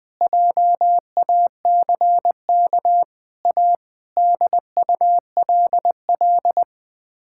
Morse code, heard as a keyed tone, about 20 wpm, 700 Hz. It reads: JACK A DULL
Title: Sets of 8 Words - 20wpm